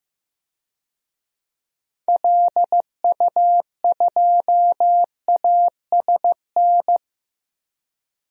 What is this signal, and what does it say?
Morse code, heard as a keyed tone, about 15 wpm, 700 Hz. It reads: LU2ASN
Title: DXCC-100 Call Signs - 6 Characters - 15wpm